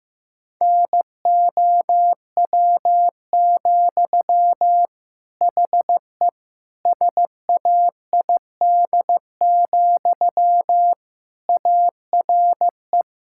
Morse code, heard as a keyed tone, about 15 wpm, 700 Hz. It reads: NOW, HE SAID, ARE
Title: Sentences from Top 1600 Words 15wpm